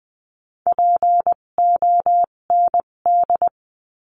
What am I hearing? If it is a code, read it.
Morse code, heard as a keyed tone, about 20 wpm, spaced as for 17 wpm, 700 Hz. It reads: POND
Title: 257 Lesson 42b - Words 14-20wpm